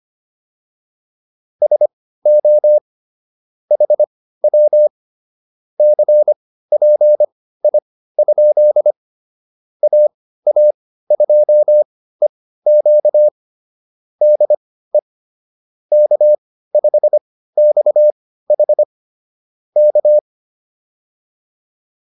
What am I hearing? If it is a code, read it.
Morse code, heard as a keyed tone, about 25 wpm, spaced as for 15 wpm, 600 Hz. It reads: SO HW CPI? AA2EQ DE K5XH K